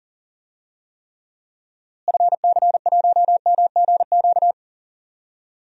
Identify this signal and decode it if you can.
Morse code, heard as a keyed tone, about 37 wpm, 700 Hz. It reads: FC1MGQ
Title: DXCC-100 Call Signs - 6 Characters - Repeated 2x - 40wpm